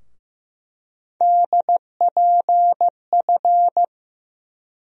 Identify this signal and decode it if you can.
Morse code, heard as a keyed tone, about 15 wpm, 700 Hz. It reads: DPF
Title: USA Call Sign Postfixes - 3 Letters 15wpm